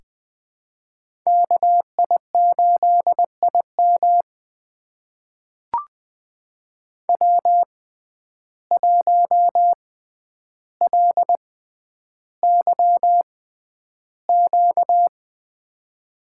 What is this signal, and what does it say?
Morse code, heard as a keyed tone, about 20 wpm, spaced as for 10 wpm, 700 Hz. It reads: KI8IM W1LYQ
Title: USA Call Signs - 5 Characters - 6x Character Spacing - 20wpm